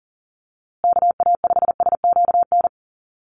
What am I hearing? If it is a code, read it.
Morse code, heard as a keyed tone, about 40 wpm, 700 Hz. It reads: KA5SQN
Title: USA Call Signs Repeated 3x 40wpm